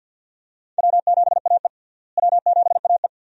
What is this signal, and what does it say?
Morse code, heard as a keyed tone, about 48 wpm, 700 Hz. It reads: W7AE W7AE
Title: USA Call Signs - 4 Characters - Repeated 2x - 50wpm